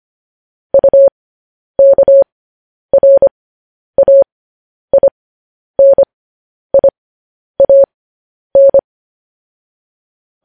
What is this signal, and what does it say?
Morse code, heard as a keyed tone, about 25 wpm, spaced as for 10 wpm, 550 Hz. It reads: UKRAINIAN
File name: podcast / News headlines-1x-25x10